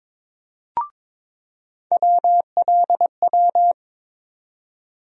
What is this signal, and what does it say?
Morse code, heard as a keyed tone, about 22 wpm, 700 Hz. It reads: WLW
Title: USA Call Sign Postfixes - 3 Letters 22wpm